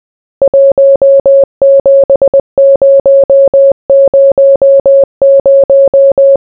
Morse code, heard as a keyed tone, about 20 wpm, 550 Hz. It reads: 17000